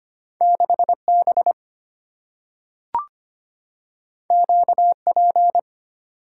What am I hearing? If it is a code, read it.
Morse code, heard as a keyed tone, about 25 wpm, 700 Hz. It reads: 6 B Q P
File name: Two Letter-Number Combinations - 25wpm